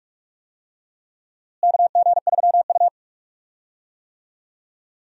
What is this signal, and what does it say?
Morse code, heard as a keyed tone, about 46 wpm, 700 Hz. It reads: KG3U
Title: USA Call Signs - 4 Characters - Repeated 2x - 45wpm